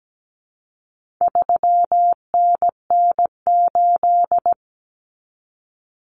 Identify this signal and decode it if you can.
Morse code, heard as a keyed tone, about 17 wpm, 700 Hz. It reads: 3NN8
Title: Common Cut Numbers - 4-Digit 17wpm